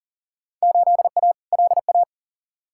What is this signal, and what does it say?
Morse code, heard as a keyed tone, about 38 wpm, 700 Hz. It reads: 8A LA
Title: Field Day Exchanges Encore - 40wpm